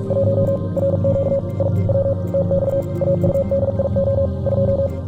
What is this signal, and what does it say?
Morse code, heard as a keyed tone, about 46 wpm, 550 Hz, with noise. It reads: 1 RQ I W TX AUDIO 2